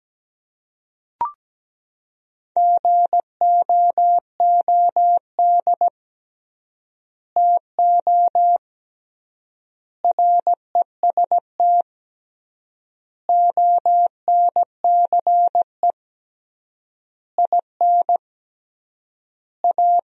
Morse code, heard as a keyed tone, about 17 wpm, 700 Hz. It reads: GOOD TO REST ONCE IN A
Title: 3x Word-Spacing - Sentences from Top 500 Words - 17wpm